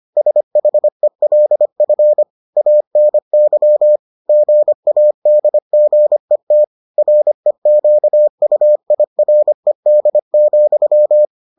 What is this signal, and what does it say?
Morse code, heard as a keyed tone, about 25 wpm, 600 Hz. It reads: SHELF ANY GADGET REQUIRED,